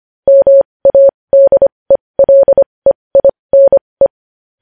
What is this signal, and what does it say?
Morse code, heard as a keyed tone, about 25 wpm, spaced as for 20 wpm, 550 Hz. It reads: MADELEINE
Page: news-headlines-in-morse-code-at-20-wpm